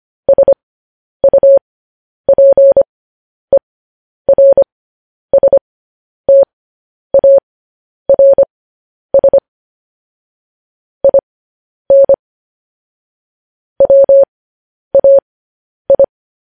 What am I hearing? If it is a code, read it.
Morse code, heard as a keyed tone, about 25 wpm, spaced as for 10 wpm, 550 Hz. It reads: SUPERSTARS IN WAI
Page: News Headlines in Morse Code at 10 WPM